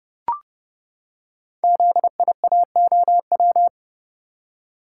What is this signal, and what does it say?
Morse code, heard as a keyed tone, about 30 wpm, 700 Hz. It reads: ZIAOW
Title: Code Groups - 5 Letter:Numbers - 30wpm